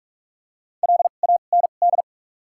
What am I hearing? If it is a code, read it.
Morse code, heard as a keyed tone, about 44 wpm, spaced as for 32 wpm, 700 Hz. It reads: LAND
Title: 2x Character Spacing - Top 200 Words - 45wpm